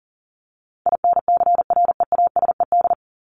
Morse code, heard as a keyed tone, about 40 wpm, 700 Hz. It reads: INCREASED
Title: ED Suffix - Top 100 Words with Base Words 40wpm